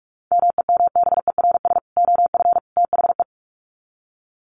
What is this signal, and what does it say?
Morse code, heard as a keyed tone, about 44 wpm, 700 Hz. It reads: MEMBERS OF THE
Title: Sets of 3 Words 45wpm